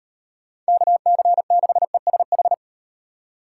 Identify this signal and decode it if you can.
Morse code, heard as a keyed tone, about 37 wpm, 700 Hz. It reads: KC6ESH